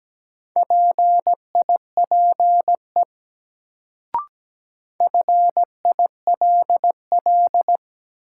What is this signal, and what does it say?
Morse code, heard as a keyed tone, about 17 wpm, 700 Hz. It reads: PIPE FILL
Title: Sentences from Top 1500 Words - New Word Drill Alt - 17wpm